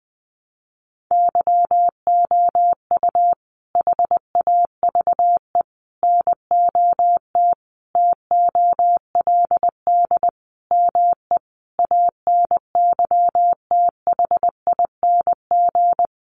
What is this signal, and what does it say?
Morse code, heard as a keyed tone, about 20 wpm, 700 Hz. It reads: YOU HAVE NOT TOLD ME ANYTHING
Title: Sentences from Top 1000 Words 20wpm